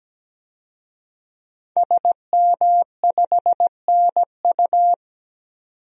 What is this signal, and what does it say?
Morse code, heard as a keyed tone, about 17 wpm, 700 Hz. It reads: SM5NU
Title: DXCC-100 Call Signs - 5 Characters - 17wpm